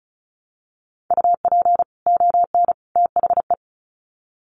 Morse code, heard as a keyed tone, about 34 wpm, 700 Hz. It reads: UP ON THE
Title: Sets of 3 Words 35wpm